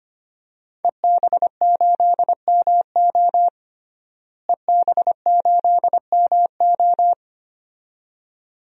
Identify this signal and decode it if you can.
Morse code, heard as a keyed tone, about 25 wpm, 700 Hz. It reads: EB8MO EB8MO
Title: DXCC-100 Call Signs - 5 Characters - Repeated 2x - 25wpm